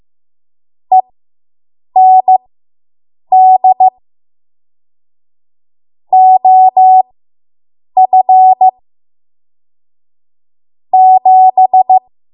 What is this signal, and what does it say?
Morse code, heard as a keyed tone, about 15 wpm, spaced as for 7 wpm, 750 Hz. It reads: END OF 7